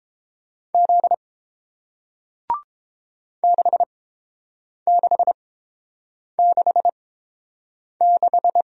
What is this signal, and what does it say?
Morse code, heard as a keyed tone, about 30 wpm, spaced as for 7 wpm, 700 Hz. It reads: Z 6666